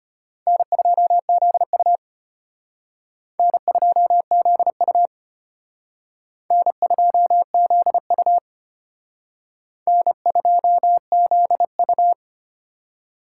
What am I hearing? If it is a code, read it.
Morse code, heard as a keyed tone, about 32 wpm, 700 Hz. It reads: N2ZU N2ZU N2ZU N2ZU